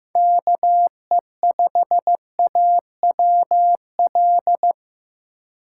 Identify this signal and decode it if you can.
Morse code, heard as a keyed tone, about 15 wpm, 700 Hz. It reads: KE5AWL
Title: USA Call Signs - 6 Characters - 2x Character Spacing - 15wpm